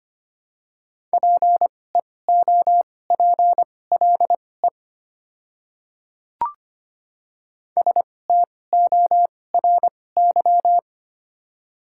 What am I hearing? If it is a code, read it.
Morse code, heard as a keyed tone, about 25 wpm, spaced as for 18 wpm, 700 Hz. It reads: PEOPLE STORY
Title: 2x Character Spacing - Top 300 Words - 25wpm